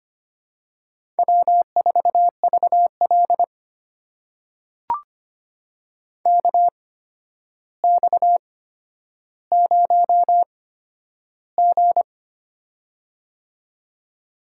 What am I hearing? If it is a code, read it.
Morse code, heard as a keyed tone, about 25 wpm, spaced as for 11 wpm, 700 Hz. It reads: W4VL KX0G
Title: USA Call Signs - 4 Characters - 8x Character Spacing - 25wpm